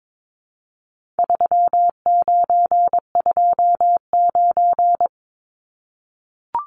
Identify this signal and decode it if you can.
Morse code, heard as a keyed tone, about 22 wpm, 700 Hz. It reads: 3929
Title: Numbers - 4-Digit 22wpm